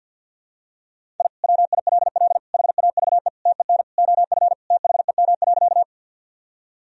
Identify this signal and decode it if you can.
Morse code, heard as a keyed tone, about 53 wpm, 700 Hz. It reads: I WILL HAVE TEN OF THEM.
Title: Sentences from Top 1000 Words - Review Part 2 - 50wpm